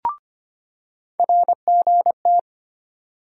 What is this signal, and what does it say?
Morse code, heard as a keyed tone, about 25 wpm, 700 Hz. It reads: RGT
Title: USA Call Sign Postfixes - 3 Letters 25wpm